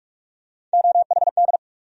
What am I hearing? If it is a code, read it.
Morse code, heard as a keyed tone, about 45 wpm, 700 Hz. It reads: OHD
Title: USA Call Sign Postfixes - 3 Letters 45wpm